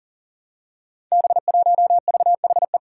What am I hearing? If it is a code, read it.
Morse code, heard as a keyed tone, about 40 wpm, 700 Hz. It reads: B1VHE
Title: Five Challenging Letter-Number Combinations 40wpm